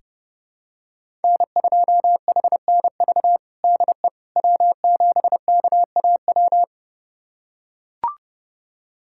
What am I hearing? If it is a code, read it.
Morse code, heard as a keyed tone, about 30 wpm, 700 Hz. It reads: N2HNV DE W7KAW
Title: USA Callsign DE Callsign - 5 Characters - 30wpm